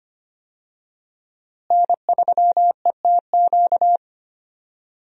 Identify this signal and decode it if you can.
Morse code, heard as a keyed tone, about 25 wpm, 700 Hz. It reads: N3ETQ
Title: USA Call Signs - 5 Characters - Repeated 2x - 25wpm